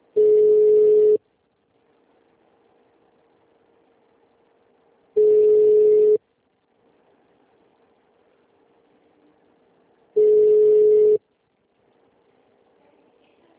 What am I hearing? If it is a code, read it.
Ringback tone (Europe, 425 Hz).
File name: out- -2009- - - .170